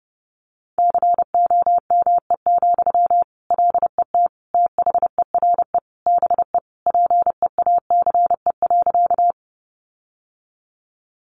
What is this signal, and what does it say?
Morse code, heard as a keyed tone, about 30 wpm, 700 Hz. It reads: COME, LET THERE BE PEACE.
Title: Sentences from Top 1500 Words 30wpm